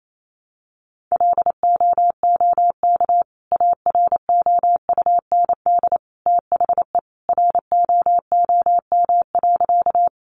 Morse code, heard as a keyed tone, about 28 wpm, 700 Hz. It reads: LOOK AROUND THE ROOM.
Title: Sentences from Top 600 Words 28wpm